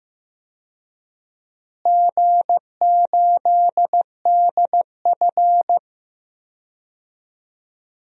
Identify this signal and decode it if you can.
Morse code, heard as a keyed tone, about 15 wpm, 700 Hz. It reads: G8DF
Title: DXCC-100 Call Signs - 4 Characters - Repeated 3x - 15wpm